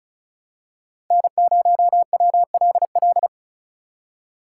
Morse code, heard as a keyed tone, about 34 wpm, 700 Hz. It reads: N0WLL
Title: USA Call Signs - 3x Character Spacing - 35wpm